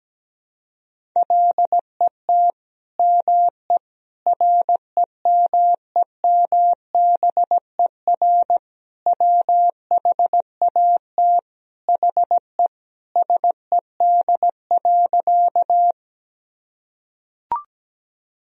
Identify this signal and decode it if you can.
Morse code, heard as a keyed tone, about 17 wpm, 700 Hz. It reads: LET ME REMEMBER WHAT HE SED.
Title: Sentences from Top 400 Words and Abbreviations 17wpm